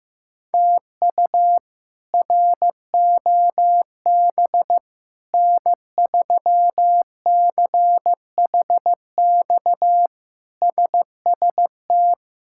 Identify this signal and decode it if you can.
Morse code, heard as a keyed tone, about 15 wpm, 700 Hz. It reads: TU ROB N3CHX SST
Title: K1USN SST QSO Practice - 15wpm